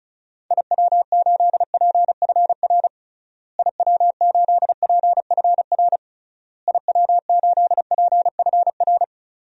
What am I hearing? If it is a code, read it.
Morse code, heard as a keyed tone, about 36 wpm, 700 Hz. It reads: IW8PFR IW8PFR IW8PFR
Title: DXCC-100 Call Signs - 6 Characters - Repeated 3x - 35wpm